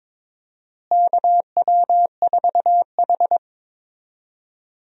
Morse code, heard as a keyed tone, about 22 wpm, 700 Hz. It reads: KW4H
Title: USA Call Signs - 4 Characters - Repeated 3x - 22wpm